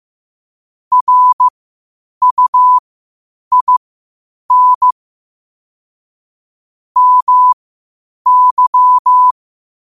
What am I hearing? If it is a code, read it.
Morse code, heard as a keyed tone, about 15 wpm, spaced as for 8 wpm, 1000 Hz. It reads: RUIN MY
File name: qotdfiles / 8wpm